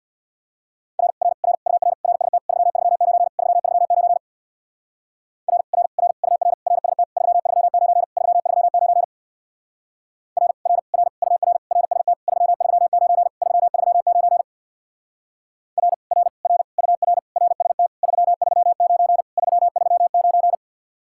Morse code, heard as a keyed tone, about 62 wpm, 700 Hz. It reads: R R R UR RST 339 339 R R R UR RST 339 339 R R R UR RST 339 339 R R R UR RST 339 339
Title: Speed-Racing Common QSO Phrases - 50wpm